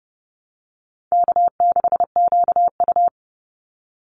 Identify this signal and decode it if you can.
Morse code, heard as a keyed tone, about 30 wpm, 700 Hz. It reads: K6QU
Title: Easy Call Signs Repeated 3x - 30wpm